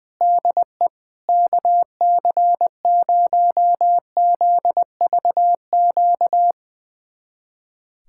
Morse code, heard as a keyed tone, about 21 wpm, 700 Hz. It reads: DE KC0ZVQ
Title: USA Callsign DE Callsign 20wpm